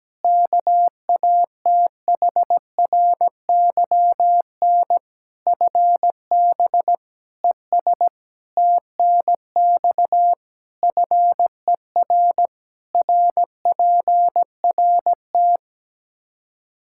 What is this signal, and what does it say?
Morse code, heard as a keyed tone, about 17 wpm, 700 Hz. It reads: KATHRYN FB ES TNX FER RPRT
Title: Practice QSO - Exchange 2 - Info - 17wpm